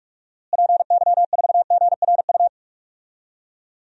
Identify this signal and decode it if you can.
Morse code, heard as a keyed tone, about 43 wpm, 700 Hz. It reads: PY4GRU
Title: DXCC-100 Call Signs - 6 Characters - Repeated 2x - 45wpm